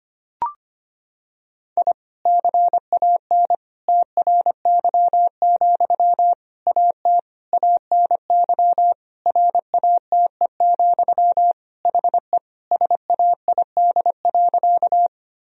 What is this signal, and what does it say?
Morse code, heard as a keyed tone, about 25 wpm, 700 Hz. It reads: I CAN TRY, AT ANY RATE, HE SAID.
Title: Sentences from Top 1000 Words - Review Part 4 - 25wpm